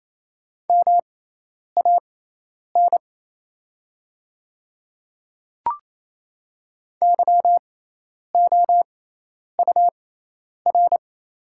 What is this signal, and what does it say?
Morse code, heard as a keyed tone, about 28 wpm, spaced as for 10 wpm, 700 Hz. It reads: MAN YOUR
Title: 6x Character Spacing - Top 200 Words - 28wpm